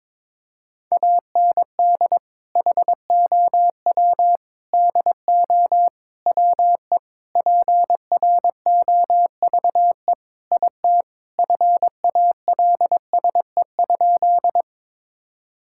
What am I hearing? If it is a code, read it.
Morse code, heard as a keyed tone, about 22 wpm, 700 Hz. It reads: AND HOW DO WE PROVE IT FALSE?
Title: Sentences from Top 1600 Words 22wpm